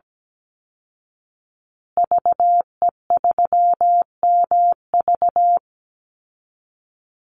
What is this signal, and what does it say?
Morse code, heard as a keyed tone, about 17 wpm, 700 Hz. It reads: VE3MV
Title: Call Signs Repeated 3x - 17wpm